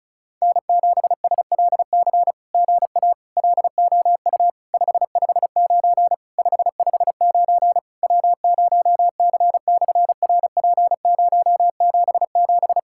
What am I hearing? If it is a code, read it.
Morse code, heard as a keyed tone, about 36 wpm, 700 Hz. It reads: N7SLC GA LOU 559 559 W0C/RP077
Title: SOTA QSO Practice 35wpm